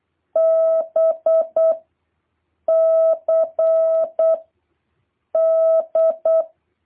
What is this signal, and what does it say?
Morse code, heard as a keyed tone, about 8 wpm, 650 Hz. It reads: B C D